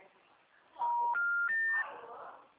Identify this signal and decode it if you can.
Special information tone, with noise.
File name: out- -2011- - - .82754